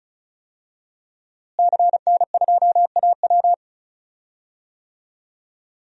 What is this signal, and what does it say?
Morse code, heard as a keyed tone, about 36 wpm, 700 Hz. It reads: CN2AW
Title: DXCC-100 Call Signs - 5 Characters - Repeated 2x - 35wpm